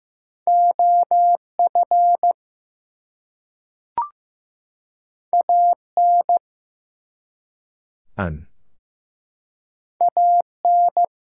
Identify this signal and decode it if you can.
Morse code, heard as a keyed tone, about 15 wpm, 700 Hz. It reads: OF AN AN
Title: Two Letter Words - 15wpm